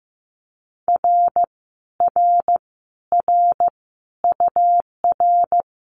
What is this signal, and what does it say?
Morse code, heard as a keyed tone, about 15 wpm, 700 Hz. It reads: R R R UR
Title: Common QSO Phrases Encore 15wpm